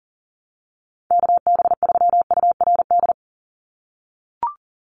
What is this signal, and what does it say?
Morse code, heard as a keyed tone, about 40 wpm, 700 Hz. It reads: KB3URD
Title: USA Call Signs 40wpm